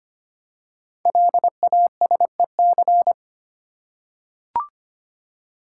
Morse code, heard as a keyed tone, about 25 wpm, 700 Hz. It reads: LASEC
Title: Code Groups - 5 Letters - 25wpm